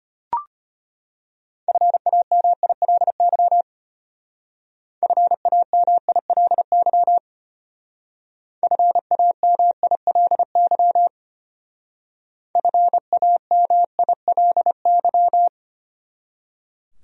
Morse code, heard as a keyed tone, about 33 wpm, 700 Hz. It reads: FAMILY FAMILY FAMILY FAMILY